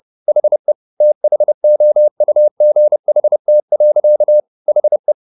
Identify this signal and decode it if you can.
Morse code, heard as a keyed tone, about 30 wpm, 600 Hz. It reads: HE THOUGHT. HE
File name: the old man and the sea / the old man and the sea 1 30wpm